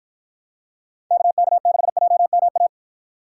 Morse code, heard as a keyed tone, about 54 wpm, 700 Hz. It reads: KK6JMA